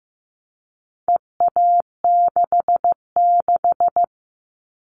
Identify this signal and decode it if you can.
Morse code, heard as a keyed tone, about 15 wpm, 700 Hz. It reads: EA66
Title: All Cut Numbers - 4-Digit 15wpm